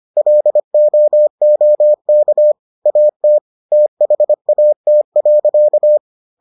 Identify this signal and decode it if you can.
Morse code, heard as a keyed tone, about 25 wpm, 600 Hz. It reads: LOOK AT THAT.